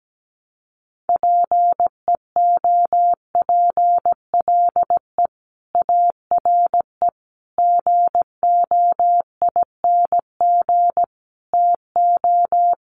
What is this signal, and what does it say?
Morse code, heard as a keyed tone, about 17 wpm, 700 Hz. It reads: PEOPLE ARE GOING TO
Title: Sets of 5 Words Encore 17wpm